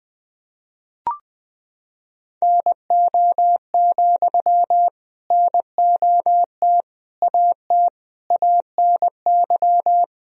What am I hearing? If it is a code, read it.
Morse code, heard as a keyed tone, about 20 wpm, 700 Hz. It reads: NO, NOT AT ANY
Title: Sentences from Top 1000 Words - Review Part 4 - 20wpm